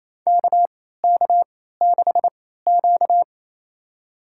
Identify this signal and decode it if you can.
Morse code, heard as a keyed tone, about 28 wpm, spaced as for 16 wpm, 700 Hz. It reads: KK6Q
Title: USA Call Signs - 4 Characters - 3x Character Spacing - 28wpm